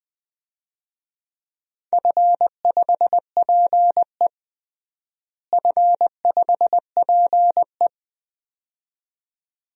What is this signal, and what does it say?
Morse code, heard as a keyed tone, about 20 wpm, 700 Hz. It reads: F5PE F5PE
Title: DXCC-100 Call Signs - 4 Characters - Repeated 2x - 20wpm